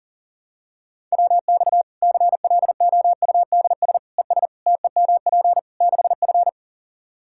Morse code, heard as a keyed tone, about 40 wpm, 700 Hz. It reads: WX CLOUDS ES TEMP 6F
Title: Practice QSO - Exchange 2 - Info - 40wpm